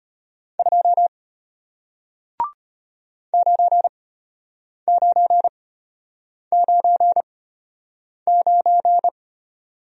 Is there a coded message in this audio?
Yes: Morse code from 0.6 to 9.1 s, heard as a keyed tone, about 33 wpm, spaced as for 8 wpm, 700 Hz. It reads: 2 9999